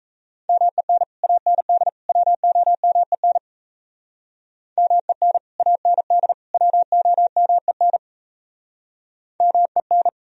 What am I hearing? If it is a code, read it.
Morse code, heard as a keyed tone, about 38 wpm, 700 Hz. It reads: MEN AND WOMEN MEN AND WOMEN MEN